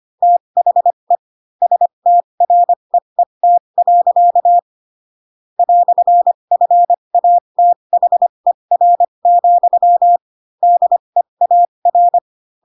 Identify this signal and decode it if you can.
Morse code, heard as a keyed tone, about 25 wpm, spaced as for 22 wpm, 700 Hz. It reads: THE STREET. "FATHER, DEAR
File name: Pinocchio / Pinocchio ch7 18 25 ik1hge